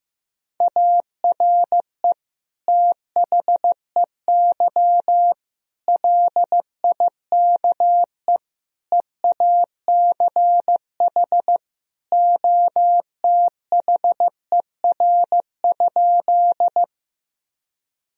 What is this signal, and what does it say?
Morse code, heard as a keyed tone, about 15 wpm, 700 Hz. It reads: ARE THEY LIKE EACH OTHER?
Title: Sentences from Top 1000 Words - Review Part 1 - 15wpm